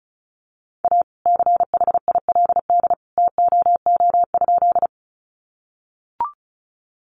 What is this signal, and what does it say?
Morse code, heard as a keyed tone, about 34 wpm, 700 Hz. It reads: A CHILD TOO?
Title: Sentences from Top 800 Words 35wpm